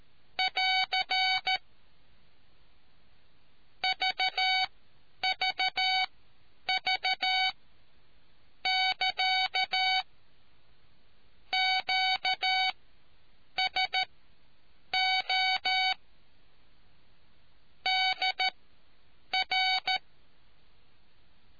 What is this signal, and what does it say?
Morse code, heard as a keyed tone, about 13 wpm, spaced as for 7 wpm, 750 Hz. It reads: <AR> VVV<KA> QSO DR